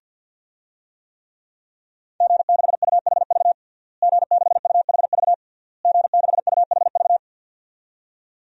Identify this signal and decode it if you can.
Morse code, heard as a keyed tone, about 49 wpm, 700 Hz. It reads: G6UHV G6UHV G6UHV
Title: DXCC-100 Call Signs - 5 Characters - Repeated 3x - 50wpm